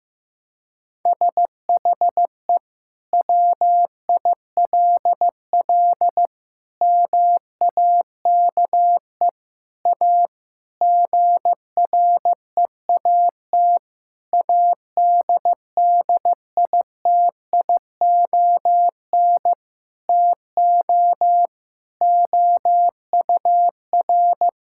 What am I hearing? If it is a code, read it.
Morse code, heard as a keyed tone, about 15 wpm, 700 Hz. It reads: SHE WILL MAKE A GREAT ADDITION TO OUR
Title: Sentences from Top 1000 Words - Review Part 6 - 15wpm